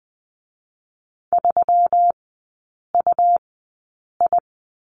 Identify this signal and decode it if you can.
Morse code, heard as a keyed tone, about 19 wpm, spaced as for 8 wpm, 700 Hz. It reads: 3UI
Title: 127 Lesson 21e - Number 6 - Three Characters 20wpm